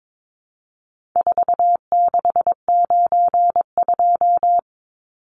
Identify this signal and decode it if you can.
Morse code, heard as a keyed tone, about 22 wpm, 700 Hz. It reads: 4692
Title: Numbers - 4-Digit 22wpm